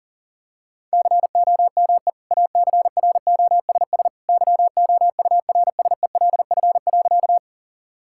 Morse code, heard as a keyed tone, about 40 wpm, 700 Hz. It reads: COME ACROSS YOURSELF.
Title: Sentences from Top 1000 Words - Review Part 5 - 40wpm